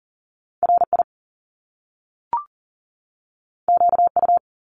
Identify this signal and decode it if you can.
Morse code, heard as a keyed tone, about 42 wpm, 700 Hz. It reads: RI QU
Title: Top Two Letter Combinations in Words 40wpm